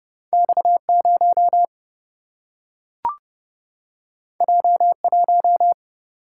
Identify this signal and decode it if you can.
Morse code, heard as a keyed tone, about 30 wpm, 700 Hz. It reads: X0 J1
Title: Single-Letter Single-Number 30wpm